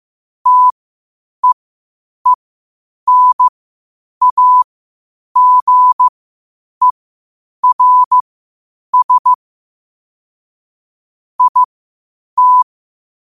Morse code, heard as a keyed tone, about 15 wpm, spaced as for 8 wpm, 1000 Hz. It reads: TEENAGERS IT
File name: qotdfiles / 8wpm